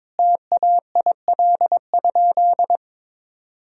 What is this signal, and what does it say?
Morse code, heard as a keyed tone, about 21 wpm, 700 Hz. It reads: TAIL?
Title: Sentences from Top 1000 Words - Review Part 3 - 22wpm